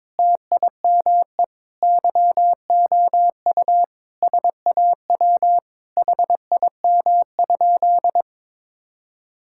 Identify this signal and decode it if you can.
Morse code, heard as a keyed tone, about 22 wpm, 700 Hz. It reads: TIME YOU SAW HIM?
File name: practice / 22wpm / Sentences with 7-Word Phrases - 22wpm